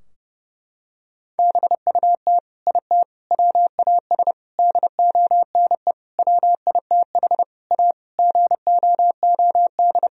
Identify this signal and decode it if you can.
Morse code, heard as a keyed tone, about 30 wpm, 700 Hz. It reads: BUT IT WAS DONE WITH A GOOD
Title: Sentences from Top 1000 Words - Review Part 4 - 30wpm